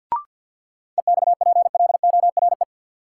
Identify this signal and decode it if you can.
Morse code, heard as a keyed tone, about 48 wpm, 700 Hz. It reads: EXPLORE
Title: Sentences from Top 1100 Words - New Word Drill - 50wpm